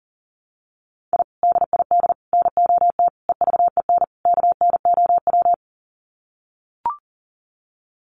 Morse code, heard as a keyed tone, about 40 wpm, 700 Hz. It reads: I DID NOT EVEN KNOW
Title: Sets of 5 Words Encore 40wpm